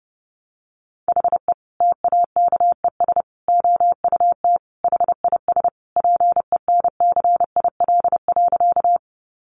Morse code, heard as a keyed tone, about 30 wpm, 700 Hz. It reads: HE TAKES OUT HIS PENCIL.
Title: Additional Easy Phrases 30wpm